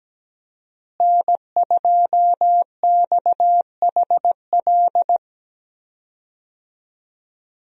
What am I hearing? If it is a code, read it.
Morse code, heard as a keyed tone, about 17 wpm, 700 Hz. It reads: N2XHL
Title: USA Call Signs - 5 Characters - Repeated 2x - 17wpm